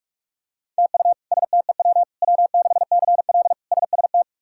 Morse code, heard as a keyed tone, about 41 wpm, 700 Hz. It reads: TU STEW W6CL SST
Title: K1USN SST QSO Practice - 45wpm